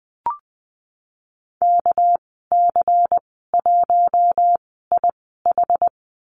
Morse code, heard as a keyed tone, about 20 wpm, spaced as for 15 wpm, 700 Hz. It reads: KC1IH
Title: 213 Lesson 34g - Number 8 - Call Signs 20wpm